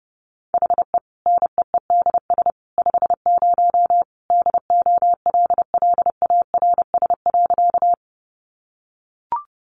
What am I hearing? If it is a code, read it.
Morse code, heard as a keyed tone, about 30 wpm, 700 Hz. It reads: HE NEEDS 50 DOLLARS.
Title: Additional Easy Phrases 30wpm